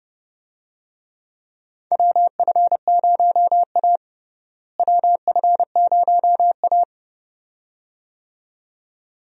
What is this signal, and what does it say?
Morse code, heard as a keyed tone, about 30 wpm, 700 Hz. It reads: WF0A WF0A
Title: USA Call Signs - 4 Characters - Repeated 2x - 30wpm